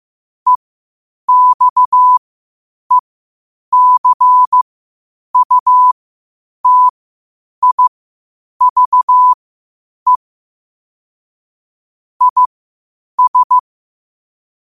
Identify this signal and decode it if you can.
Morse code, heard as a keyed tone, about 15 wpm, spaced as for 8 wpm, 1000 Hz. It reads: EXECUTIVE IS
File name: qotdfiles / 8wpm